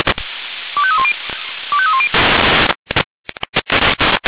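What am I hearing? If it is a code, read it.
Five-tone selective call (ZVEI), heard as quick runs of five tones, tones of about 70 ms. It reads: 25310 25310
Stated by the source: ZVEI five-tone selective call received on SDR.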